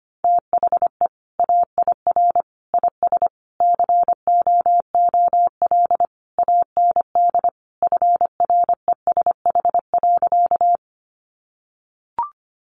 Morse code, heard as a keyed tone, about 25 wpm, 700 Hz. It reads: THE AIR IS COOL AND FRESH.
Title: Sentences from Top 1200 Words 25wpm